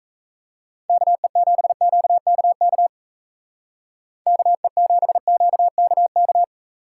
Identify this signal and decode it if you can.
Morse code, heard as a keyed tone, about 37 wpm, 700 Hz. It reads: KE7QKK KE7QKK